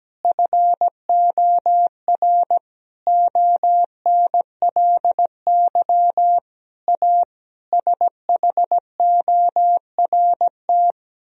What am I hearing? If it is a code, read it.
Morse code, heard as a keyed tone, about 17 wpm, 700 Hz. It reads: FOR ONLY A SHORT
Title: Sentences from Top 1000 Words - Review Part 3 - 17wpm